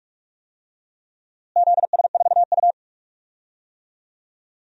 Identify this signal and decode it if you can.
Morse code, heard as a keyed tone, about 46 wpm, 700 Hz. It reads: ZS4U
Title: DXCC-100 Call Signs - 4 Characters - Repeated 2x - 45wpm